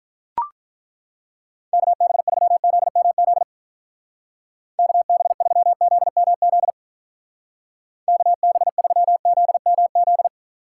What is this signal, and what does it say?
Morse code, heard as a keyed tone, about 43 wpm, 700 Hz. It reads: KB3ZMZ KB3ZMZ KB3ZMZ
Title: Speed Racing - USA 2x3 Call Signs 35wpm